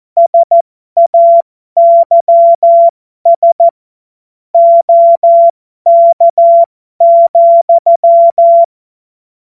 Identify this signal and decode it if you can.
Morse code, heard as a keyed tone, about 14 wpm, spaced as for 12 wpm, 680 Hz. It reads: SAYS OK,